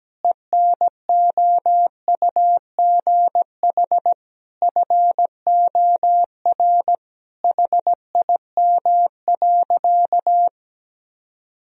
Morse code, heard as a keyed tone, about 17 wpm, 700 Hz. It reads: ENOUGH FOR HIM.
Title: Sentences from Top 1000 Words - Review Part 2 - 17wpm